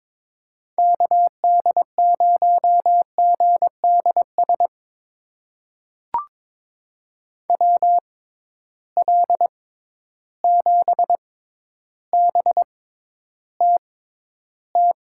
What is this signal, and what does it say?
Morse code, heard as a keyed tone, about 22 wpm, spaced as for 11 wpm, 700 Hz. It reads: KD0GDS WL7BTT
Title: USA Call Signs - 6 Characters - 6x Character Spacing - 22wpm